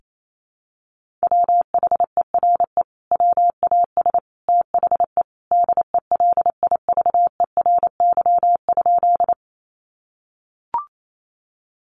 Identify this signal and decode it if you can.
Morse code, heard as a keyed tone, about 28 wpm, 700 Hz. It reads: WHERE WAS THE DELIVERY?
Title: Sentences from Top 1500 Words 28wpm